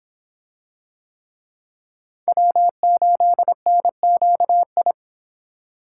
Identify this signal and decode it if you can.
Morse code, heard as a keyed tone, about 25 wpm, 700 Hz. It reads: W8NQI